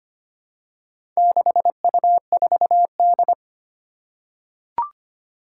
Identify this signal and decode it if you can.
Morse code, heard as a keyed tone, about 25 wpm, 700 Hz. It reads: 6U4D
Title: Four Challenging Letter-Number Combinations 25wpm